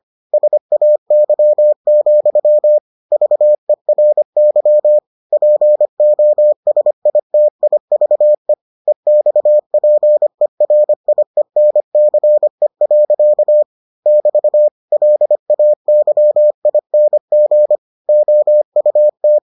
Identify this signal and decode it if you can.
Morse code, heard as a keyed tone, about 25 wpm, 600 Hz. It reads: SAY, VERY POSITIVE EXPERIENCE. <BT> LAYING OUT